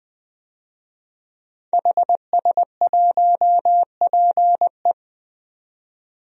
Morse code, heard as a keyed tone, about 20 wpm, 700 Hz. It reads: HS1PE
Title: DXCC-100 Call Signs - 5 Characters - 20wpm